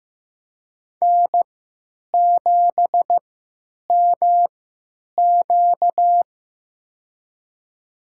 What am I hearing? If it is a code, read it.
Morse code, heard as a keyed tone, about 15 wpm, spaced as for 9 wpm, 700 Hz. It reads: N7MQ
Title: USA Call Signs - 4 Characters - 3x Character Spacing - 15wpm